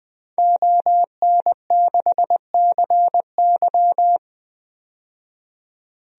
Morse code, heard as a keyed tone, about 20 wpm, 700 Hz. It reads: ON6CY
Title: DXCC-100 Call Signs - 5 Characters - 20wpm